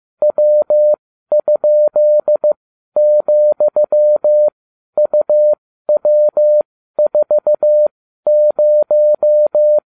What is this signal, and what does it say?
Morse code, heard as a keyed tone, about 15 wpm, spaced as for 12 wpm, 600 Hz. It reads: W?,UW40